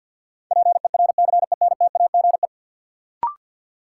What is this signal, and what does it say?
Morse code, heard as a keyed tone, about 53 wpm, 700 Hz. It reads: PERCENTAGE
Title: Sentences from Top 1300 Words - New Word Drill - 50wpm